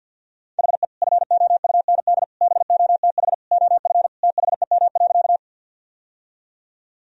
Morse code, heard as a keyed tone, about 52 wpm, 700 Hz. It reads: HE FOUND BOTH OF THEM.
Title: Sentences from Top 1000 Words - Review Part 1 - 50wpm